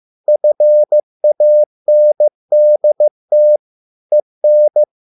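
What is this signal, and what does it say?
Morse code, heard as a keyed tone, about 15 wpm, 600 Hz. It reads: FANDT EN
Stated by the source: Morse code at a slow 15 wpm from CW med Gnister.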